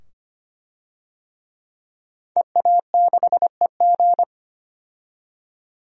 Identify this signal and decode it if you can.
Morse code, heard as a keyed tone, about 25 wpm, 700 Hz. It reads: EA6EG
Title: DXCC-100 Call Signs - 5 Characters - 25wpm